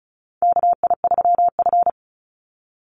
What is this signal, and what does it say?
Morse code, heard as a keyed tone, about 34 wpm, 700 Hz. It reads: KI3F
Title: Call Signs - 35wpm